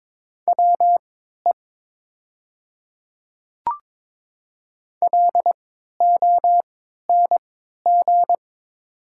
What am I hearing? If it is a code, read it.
Morse code, heard as a keyed tone, about 22 wpm, spaced as for 12 wpm, 700 Hz. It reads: WE LONG
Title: 3x Character Spacing - Top 100 Words - 22wpm